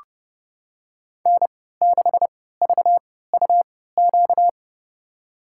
Morse code, heard as a keyed tone, about 31 wpm, spaced as for 17 wpm, 700 Hz. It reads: N6VUQ
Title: USA Call Signs - 5 Characters - 3x Character Spacing - 30wpm